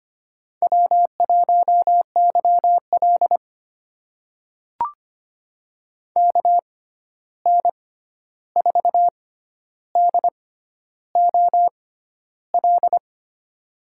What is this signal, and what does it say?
Morse code, heard as a keyed tone, about 25 wpm, spaced as for 11 wpm, 700 Hz. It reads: W1YL KN4DOL
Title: USA Call Signs - 6x Character Spacing - 25wpm